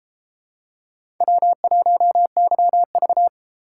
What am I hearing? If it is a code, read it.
Morse code, heard as a keyed tone, about 32 wpm, 700 Hz. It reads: W1YV